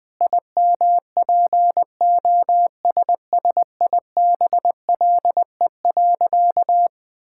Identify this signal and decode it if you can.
Morse code, heard as a keyed tone, about 21 wpm, 700 Hz. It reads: IMPOSSIBLE.
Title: Sentences with 5-Word Phrases - 20wpm